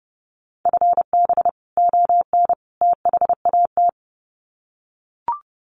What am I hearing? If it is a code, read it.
Morse code, heard as a keyed tone, about 30 wpm, 700 Hz. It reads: FB ON THAT
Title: Common QSO Phrases 30wpm